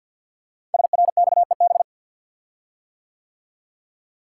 Morse code, heard as a keyed tone, about 52 wpm, 700 Hz. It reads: SRXEB
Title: Code Groups - 5 Letters - 50wpm